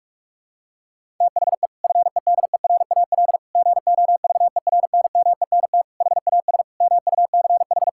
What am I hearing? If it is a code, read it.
Morse code, heard as a keyed tone, about 47 wpm, 700 Hz. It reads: THE FEDERAL GOVERNMENT HAS MUCH
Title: Sentences from Top 1000 Words - Review Part 5 - 45wpm